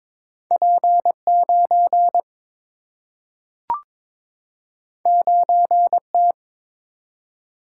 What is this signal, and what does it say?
Morse code, heard as a keyed tone, about 22 wpm, 700 Hz. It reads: P 9 9 T
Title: Single-Letter Single-Number 22wpm